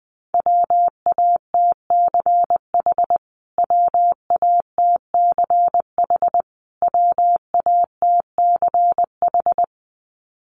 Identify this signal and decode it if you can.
Morse code, heard as a keyed tone, about 20 wpm, 700 Hz. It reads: WATCH WATCH WATCH
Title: Top 500 Words Repeated 20wpm